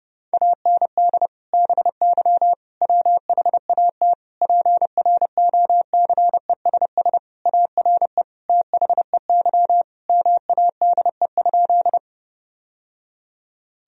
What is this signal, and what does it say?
Morse code, heard as a keyed tone, about 30 wpm, 700 Hz. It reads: AND BY WHAT PROCESS ARE THEY MADE?
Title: Sentences from Top 1000 Words - Review Part 3 - 30wpm